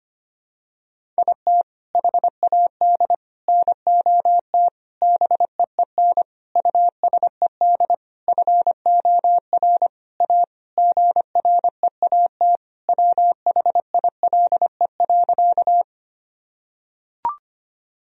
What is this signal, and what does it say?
Morse code, heard as a keyed tone, about 25 wpm, 700 Hz. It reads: IT HAD NOT BEEN USED FOR A GREAT WHILE.
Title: Sentences from Top 1000 Words - Review Part 1 - 25wpm